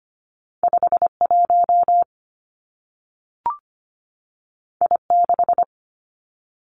Morse code, heard as a keyed tone, about 25 wpm, 700 Hz. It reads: 5 1 I 6
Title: Two Challenging Letter-Number Combinations 25wpm